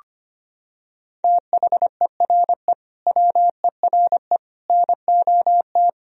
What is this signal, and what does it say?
Morse code, heard as a keyed tone, about 25 wpm, 700 Hz. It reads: THERE WERE NOT
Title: Sentences from Top 1000 Words - Review Part 1 - 25wpm